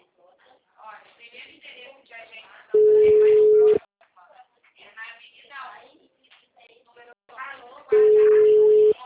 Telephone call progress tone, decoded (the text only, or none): ringback tone